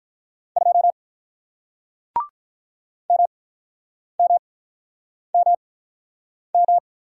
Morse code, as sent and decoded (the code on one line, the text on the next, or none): ..--- / -- -- -- --
2 MMMM